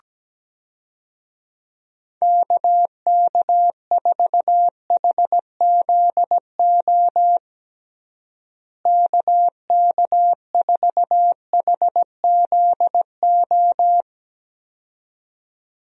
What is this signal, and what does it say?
Morse code, heard as a keyed tone, about 17 wpm, 700 Hz. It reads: KK4HZO KK4HZO
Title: USA Call Signs - 6 Characters - Repeated 2x - 17wpm